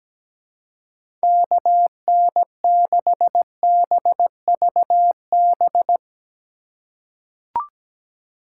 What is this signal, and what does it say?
Morse code, heard as a keyed tone, about 17 wpm, 700 Hz. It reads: KN6BVB
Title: USA Call Signs - 6 Characters - 3x Character Spacing - 17wpm